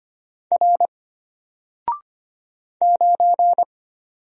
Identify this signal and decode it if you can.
Morse code, heard as a keyed tone, about 25 wpm, 700 Hz. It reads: R 9
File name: Single-Letter-Number-25wpm